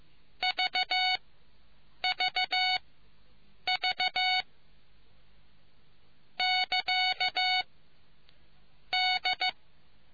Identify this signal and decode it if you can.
Morse code, heard as a keyed tone, about 15 wpm, spaced as for 7 wpm, 740 Hz. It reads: VVV <KA> D